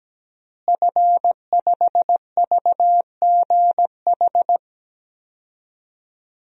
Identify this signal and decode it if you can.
Morse code, heard as a keyed tone, about 17 wpm, 700 Hz. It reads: F5VGH
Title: DXCC-100 Call Signs - 5 Characters - Repeated 3x - 17wpm